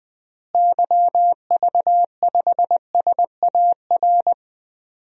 Morse code, heard as a keyed tone, about 20 wpm, 700 Hz. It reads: YV5SAR
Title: DXCC-100 Call Signs - 6 Characters - Repeated 3x - 20wpm